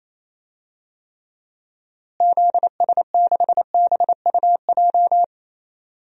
Morse code, heard as a keyed tone, about 28 wpm, 700 Hz. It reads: ZS6BUJ
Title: DXCC-100 Call Signs - 6 Characters - Repeated 3x - 28wpm